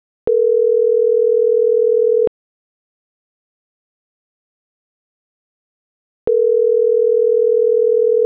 Ringback tone (North America), 440 + 480 Hz.